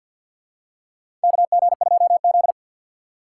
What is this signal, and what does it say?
Morse code, heard as a keyed tone, about 50 wpm, 700 Hz. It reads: KG2Z
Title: USA Call Signs - 4 Characters - 2x Character Spacing - 50wpm